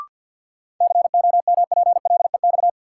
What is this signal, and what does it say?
Morse code, heard as a keyed tone, about 52 wpm, 700 Hz. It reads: COMPLEX
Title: Sentences from Top 900 Words - New Word Drill - 50wpm